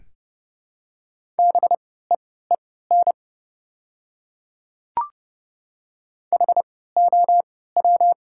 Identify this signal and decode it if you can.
Morse code, heard as a keyed tone, about 30 wpm, spaced as for 17 wpm, 700 Hz. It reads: BEEN HOW
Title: 3x Character Spacing - Top 100 Words - 30wpm